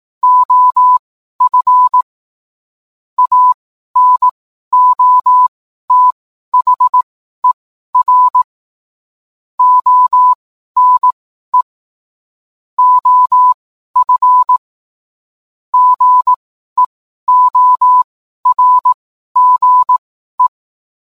Morse code, heard as a keyed tone, about 18 wpm, spaced as for 12 wpm, 1000 Hz. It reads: OF ANOTHER ONE OF GEORGE